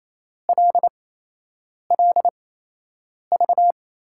Morse code, heard as a keyed tone, about 27 wpm, spaced as for 8 wpm, 700 Hz. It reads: LLV
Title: USA Call Signs - 6 Characters - 8x Character Spacing - 28wpm